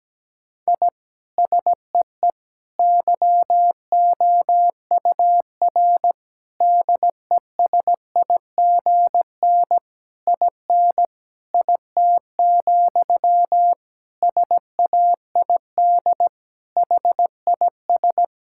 Morse code, heard as a keyed tone, about 17 wpm, 700 Hz. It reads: I SEE YOUR DESIGN IN IT, SAID HIS
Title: Sentences from Top 1000 Words - Review Part 5 - 17wpm